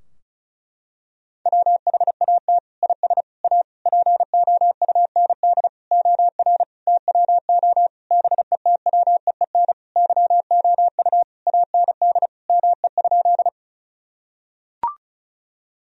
Morse code, heard as a keyed tone, about 36 wpm, 700 Hz. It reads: WHAT IS A POUND OR TWO BETWEEN YOU AND ME?
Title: Sentences from Top 1000 Words - Review Part 2 - 35wpm